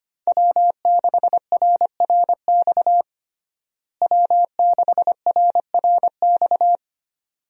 Morse code, heard as a keyed tone, about 25 wpm, 700 Hz. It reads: W6RRX W6RRX
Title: USA Call Signs - 5 Characters - Repeated 2x - 25wpm